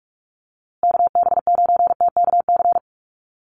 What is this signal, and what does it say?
Morse code, heard as a keyed tone, about 44 wpm, 700 Hz. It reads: KB9TKC
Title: USA Call Signs Repeated 2x 45wpm